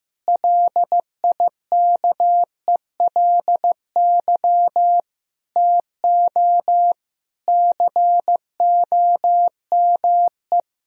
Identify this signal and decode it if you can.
Morse code, heard as a keyed tone, about 15 wpm, 700 Hz. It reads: LIKELY TO COME
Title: Sentences from Top 1000 Words - Review Part 4 - 15wpm